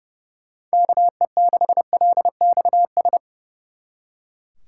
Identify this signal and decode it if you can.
Morse code, heard as a keyed tone, about 36 wpm, 700 Hz. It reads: KE6LXS
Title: Speed Racing - USA 2x3 Call Signs 30wpm